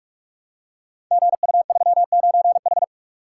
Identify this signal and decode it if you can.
Morse code, heard as a keyed tone, about 44 wpm, 700 Hz. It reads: GU39H
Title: Code Groups - 5 Letter:Numbers - 45wpm